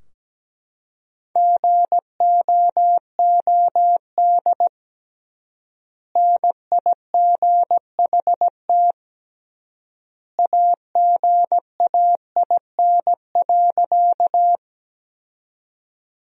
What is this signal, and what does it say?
Morse code, heard as a keyed tone, about 17 wpm, 700 Hz. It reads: GOOD NIGHT AGAIN.
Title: 3x Word-Spacing - Sentences from Top 300 Words - 17wpm